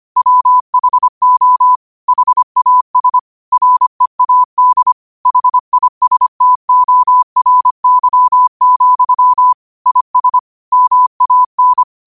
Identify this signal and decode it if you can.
Morse code, heard as a keyed tone, about 25 wpm, 1000 Hz. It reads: WHO HAS READ HISTORY, IS MAN
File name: qotdfiles / 25wpm